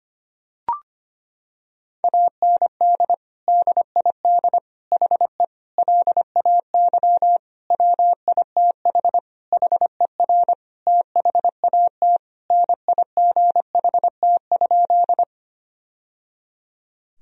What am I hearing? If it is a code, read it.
Morse code, heard as a keyed tone, about 25 wpm, 700 Hz. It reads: AND DID HE LAY WITH HER THAT NIGHT?
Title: Sentences from Top 1000 Words - Review Part 2 - 25wpm